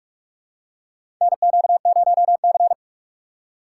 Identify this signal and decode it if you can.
Morse code, heard as a keyed tone, about 46 wpm, 700 Hz. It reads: NQ0C
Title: USA Call Signs - 2x Character Spacing - 45wpm